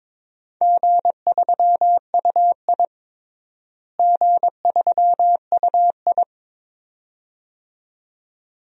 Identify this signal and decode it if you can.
Morse code, heard as a keyed tone, about 22 wpm, 700 Hz. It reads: G3UI G3UI
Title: DXCC-100 Call Signs - 4 Characters - Repeated 2x - 22wpm